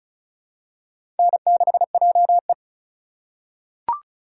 Morse code, heard as a keyed tone, about 36 wpm, 700 Hz. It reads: N6JE
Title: USA Call Signs - 2x Character Spacing - 35wpm